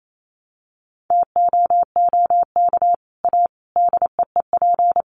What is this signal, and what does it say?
Morse code, heard as a keyed tone, about 28 wpm, 700 Hz. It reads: TOOK A DEEP
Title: Sets of 5 Words Encore 28wpm